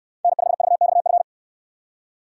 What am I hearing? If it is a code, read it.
Morse code, heard as a keyed tone, about 65 wpm, 700 Hz. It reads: N5VXF